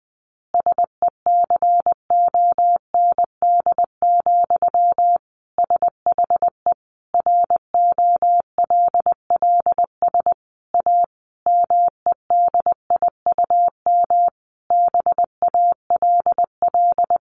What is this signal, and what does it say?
Morse code, heard as a keyed tone, about 20 wpm, 700 Hz. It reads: SECOND, SHE ROLLS A MEDIUM BALL
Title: Easy Phrases 20wpm